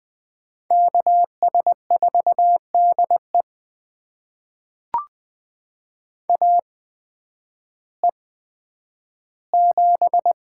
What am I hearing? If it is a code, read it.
Morse code, heard as a keyed tone, about 20 wpm, spaced as for 11 wpm, 700 Hz. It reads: KS4DE AE7
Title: USA Call Signs - 5 Characters - 8x Character Spacing - 20wpm